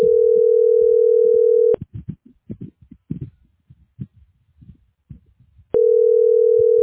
Ringback tone (North America), 440 + 480 Hz.